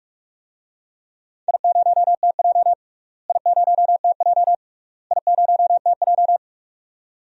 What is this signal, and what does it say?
Morse code, heard as a keyed tone, about 44 wpm, 700 Hz. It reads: I0TJ I0TJ I0TJ
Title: DXCC-100 Call Signs - 4 Characters - Repeated 3x - 45wpm